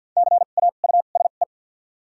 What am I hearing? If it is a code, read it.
Morse code, heard as a keyed tone, about 51 wpm, spaced as for 36 wpm, 700 Hz. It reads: CAUSE
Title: 2x Character Spacing - Top 500 Words - 50wpm